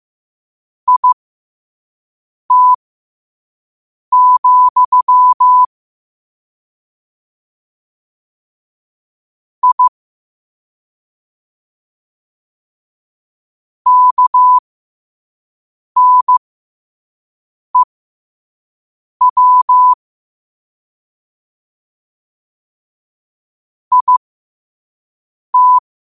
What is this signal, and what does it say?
Morse code, heard as a keyed tone, about 15 wpm, spaced as for 5 wpm, 1000 Hz. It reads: IT, I KNEW IT
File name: qotdfiles / 5wpm